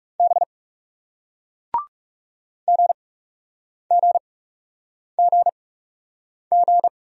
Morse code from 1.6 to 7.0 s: GGGG

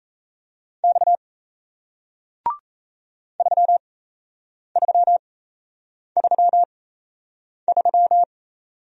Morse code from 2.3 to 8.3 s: 3333

